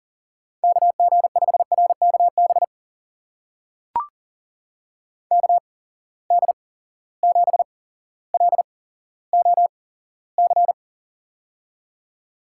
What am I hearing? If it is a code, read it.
Morse code, heard as a keyed tone, about 40 wpm, spaced as for 17 wpm, 700 Hz. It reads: KG5RKB KD7LOC
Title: USA Call Signs - 6 Characters - 8x Character Spacing - 40wpm